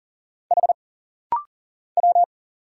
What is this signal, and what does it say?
Morse code, heard as a keyed tone, about 40 wpm, 700 Hz. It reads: H W
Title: Single Letter-Number Alt - Rapid-Fire 40wpm